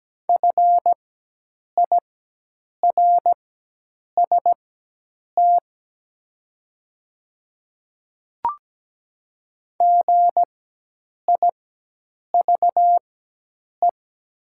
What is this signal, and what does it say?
Morse code, heard as a keyed tone, about 17 wpm, spaced as for 8 wpm, 700 Hz. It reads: FIRST GIVE
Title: 4x Character Spacing - Top 200 Words - 17wpm